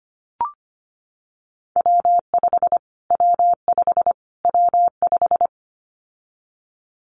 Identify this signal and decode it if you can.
Morse code, heard as a keyed tone, about 25 wpm, 700 Hz. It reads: W5 W5 W5
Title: Callsign Prefixes for Top 25 DXCC - Repeated 3x - 25wpm